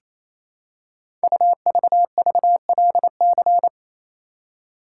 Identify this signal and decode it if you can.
Morse code, heard as a keyed tone, about 29 wpm, 700 Hz. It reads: UVVLC
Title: Code Groups - 5 Letters - 28wpm